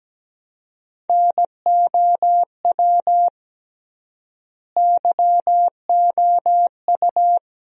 Morse code, heard as a keyed tone, about 17 wpm, 700 Hz. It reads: NOW YOU
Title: 3x Word-Spacing - Sentences from Top 100 Words - 17wpm